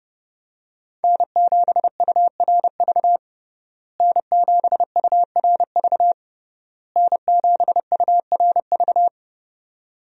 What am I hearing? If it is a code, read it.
Morse code, heard as a keyed tone, about 29 wpm, 700 Hz. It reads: N7URV N7URV N7URV
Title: USA Call Signs - 5 Characters - Repeated 3x - 30wpm